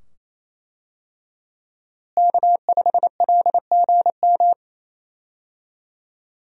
Morse code, heard as a keyed tone, about 28 wpm, 700 Hz. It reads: K5LGM
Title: USA Call Signs - 5 Characters - Repeated 2x - 28wpm